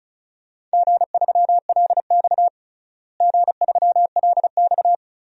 Morse code, heard as a keyed tone, about 36 wpm, 700 Hz. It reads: G3LX G3LX
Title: DXCC-100 Call Signs - 4 Characters - Repeated 3x - 35wpm